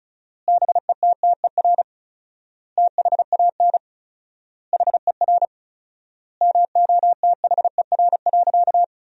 Morse code, heard as a keyed tone, about 36 wpm, 700 Hz. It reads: BETTER THAN HER MOTHER.
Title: 4x Word-Spacing - Sentences from Top 400 Words - 35wpm